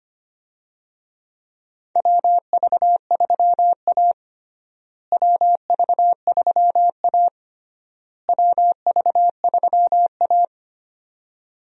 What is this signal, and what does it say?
Morse code, heard as a keyed tone, about 25 wpm, 700 Hz. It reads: WV3A WV3A WV3A
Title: USA Call Signs - 4 Characters - Repeated 3x - 25wpm